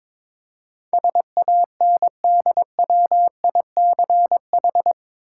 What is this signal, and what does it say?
Morse code, heard as a keyed tone, about 22 wpm, 700 Hz. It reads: SANDWICH
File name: practice / 22wpm / Eight Letter Words - 22wpm